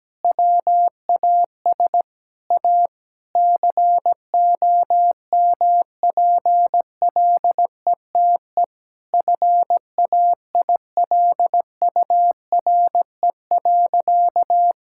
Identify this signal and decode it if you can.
Morse code, heard as a keyed tone, about 17 wpm, 700 Hz. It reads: WAS A COMPLETE FAILURE.
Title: Sentences from Top 1000 Words - Review Part 6 - 17wpm